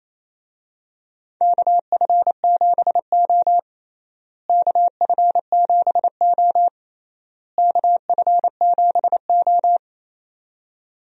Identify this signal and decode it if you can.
Morse code, heard as a keyed tone, about 27 wpm, 700 Hz. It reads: KF7O KF7O KF7O
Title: USA Call Signs - 4 Characters - Repeated 3x - 28wpm